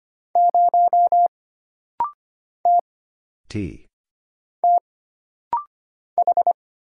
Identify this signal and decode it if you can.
Morse code, heard as a keyed tone, about 25 wpm, spaced as for 6 wpm, 700 Hz. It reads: 0 T TH
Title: Single Letter-Number - Rapid-Fire 25wpm